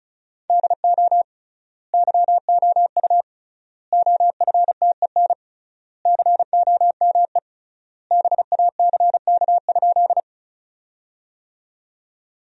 Morse code, heard as a keyed tone, about 34 wpm, 700 Hz. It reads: DO YOU OFTEN COME BACK?
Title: 3x Word-Spacing - Sentences from Top 300 Words - 35wpm